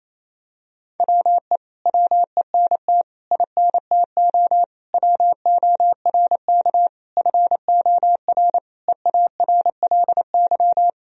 Morse code, heard as a keyed tone, about 28 wpm, 700 Hz. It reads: WE WENT INTO WORK FOR EARLY
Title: Sentences from Top 1000 Words - Review Part 4 - 28wpm